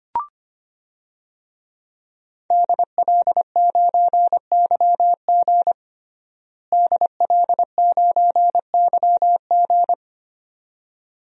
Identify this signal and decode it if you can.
Morse code, heard as a keyed tone, about 25 wpm, 700 Hz. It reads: DL9YG DL9YG
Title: DXCC-100 Call Signs - 5 Characters - Repeated 2x - 25wpm